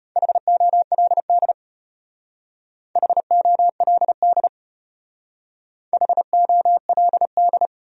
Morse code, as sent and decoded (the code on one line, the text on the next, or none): .... --- .-.. -.. / .... --- .-.. -.. / .... --- .-.. -..
HOLD HOLD HOLD